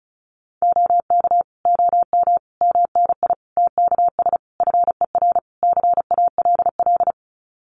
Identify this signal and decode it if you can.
Morse code, heard as a keyed tone, about 34 wpm, 700 Hz. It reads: OK OM MNI TKS FER CALL
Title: Common QSO Phrases 35wpm